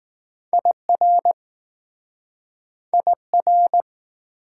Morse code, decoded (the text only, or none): IR IR